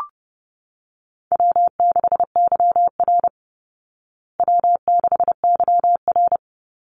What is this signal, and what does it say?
Morse code, heard as a keyed tone, about 30 wpm, 700 Hz. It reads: W6YR W6YR
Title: Easy Call Signs Repeated 3x - 30wpm